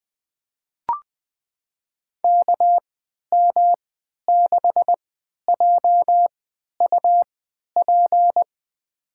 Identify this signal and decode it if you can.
Morse code, heard as a keyed tone, about 20 wpm, spaced as for 11 wpm, 700 Hz. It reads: KM6JUP
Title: USA Call Signs - 6 Characters - 3x Character Spacing - 20wpm